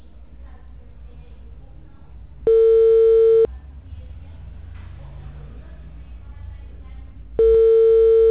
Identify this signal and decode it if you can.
Ringback tone (China, 450 Hz), with speech.
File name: out- -2020- - - .61